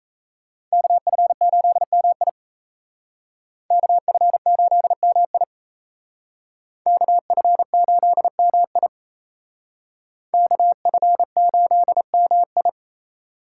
Morse code, heard as a keyed tone, about 35 wpm, 700 Hz. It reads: KF8MI KF8MI KF8MI KF8MI